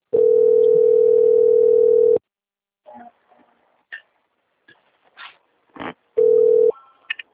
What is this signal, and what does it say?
Ringback tone (North America), 440 + 480 Hz.
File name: out- - - - - .12631